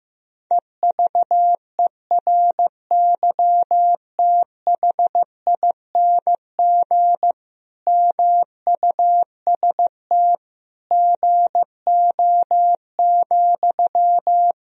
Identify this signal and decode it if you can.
Morse code, heard as a keyed tone, about 16 wpm, 700 Hz. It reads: EVERYTHING MUST GO,
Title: Sentences from Top 1000 Words - Review Part 4 - 15wpm